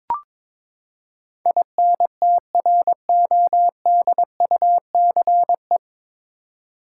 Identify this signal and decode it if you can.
Morse code, heard as a keyed tone, about 22 wpm, 700 Hz. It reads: INTRODUCE
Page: Morse Code Ninja practice 22wpm